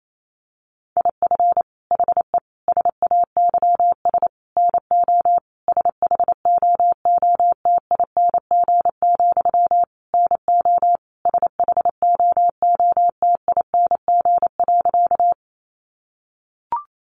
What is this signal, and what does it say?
Morse code, heard as a keyed tone, about 28 wpm, 700 Hz. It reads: IF HE SAYS NO SHOOTING, NO SHOOTING.
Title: A Sound Of Thunder 28wpm